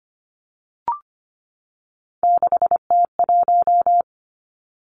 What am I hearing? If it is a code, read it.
Morse code, heard as a keyed tone, about 25 wpm, 700 Hz. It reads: 6T1
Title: Three Challenging Letter-Number Combinations 25wpm